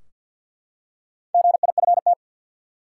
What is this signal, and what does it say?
Morse code, heard as a keyed tone, about 52 wpm, 700 Hz. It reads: GIFT
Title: Four Letter Words - 50wpm